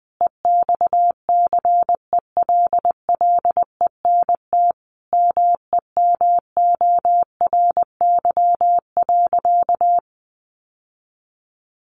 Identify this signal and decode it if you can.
Morse code, heard as a keyed tone, about 19 wpm, 700 Hz. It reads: EXCELLENT MEMORY.
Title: Sentences from Top 1300 Words 20wpm